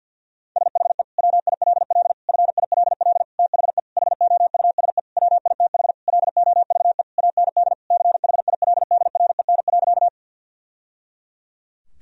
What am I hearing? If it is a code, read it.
Morse code, heard as a keyed tone, about 49 wpm, 700 Hz. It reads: SHE WILL FILL THE HOUSE WITH LOVE AND CHILDREN.
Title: Sentences from Top 1000 Words - Review Part 3 - 50wpm